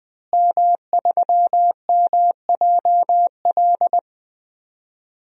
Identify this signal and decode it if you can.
Morse code, heard as a keyed tone, about 21 wpm, 700 Hz. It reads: M3MJL
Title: DXCC-100 Call Signs - 5 Characters - Repeated 2x - 20wpm